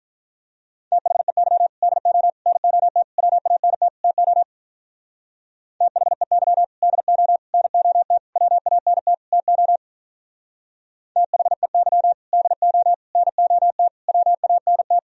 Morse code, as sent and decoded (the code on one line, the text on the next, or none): - .... -.-- / -.. --- / -. --- - / .-- .- -. - / - --- / - .... -.-- / -.. --- / -. --- - / .-- .- -. - / - --- / - .... -.-- / -.. --- / -. --- - / .-- .- -. -
THEY DO NOT WANT TO THEY DO NOT WANT TO THEY DO NOT WANT